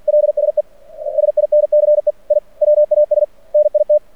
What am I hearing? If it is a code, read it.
Morse code, heard as a keyed tone, about 50 wpm, 600 Hz. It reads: LSE SINCE I WAS NIT
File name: archived / CRSnet